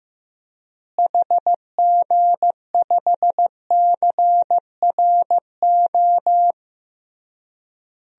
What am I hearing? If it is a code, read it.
Morse code, heard as a keyed tone, about 15 wpm, 700 Hz. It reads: HG5CRO
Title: DXCC-100 Call Signs - 6 Characters - Repeated 3x - 15wpm